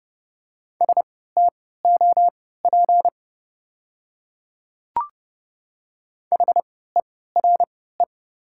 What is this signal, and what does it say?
Morse code, heard as a keyed tone, about 30 wpm, spaced as for 17 wpm, 700 Hz. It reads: STOP HERE
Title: 3x Character Spacing - Top 300 Words - 30wpm